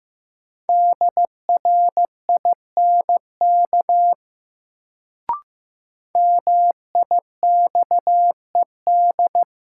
Morse code, heard as a keyed tone, about 16 wpm, 700 Hz. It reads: DRINK MIXED
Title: Sentences from Top 1500 Words - New Word Drill - 15wpm